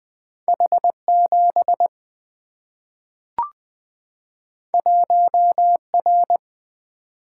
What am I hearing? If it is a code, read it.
Morse code, heard as a keyed tone, about 20 wpm, 700 Hz. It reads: H 7 1 R
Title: Single-Letter Single-Number 20wpm